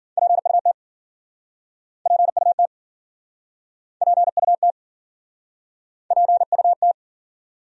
Morse code, heard as a keyed tone, about 55 wpm, 700 Hz. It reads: PUT PUT PUT PUT